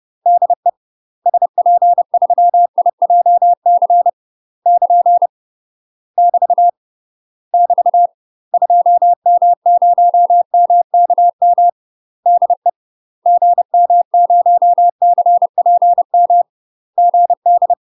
Morse code, heard as a keyed tone, about 30 wpm, 700 Hz. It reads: DE SP3IJC <KN> <BT> <BT> 2M0MKM DE GM0CPM GD
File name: UK MASTER QSOs TRAINING / 30WPM / UK MASTER QSOs 0009